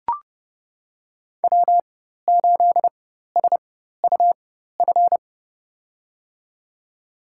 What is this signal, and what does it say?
Morse code, heard as a keyed tone, about 30 wpm, spaced as for 14 wpm, 700 Hz. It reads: W8SUF
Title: USA Call Signs - 5 Characters - 4x Character Spacing - 30wpm